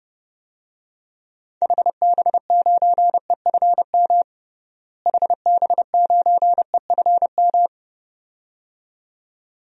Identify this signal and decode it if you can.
Morse code, heard as a keyed tone, about 31 wpm, 700 Hz. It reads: HB9EFM HB9EFM
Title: DXCC-100 Call Signs - 6 Characters - Repeated 2x - 30wpm